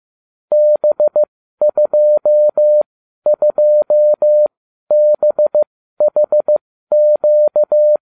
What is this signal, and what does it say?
Morse code, heard as a keyed tone, about 15 wpm, spaced as for 12 wpm, 600 Hz. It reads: B22BHQ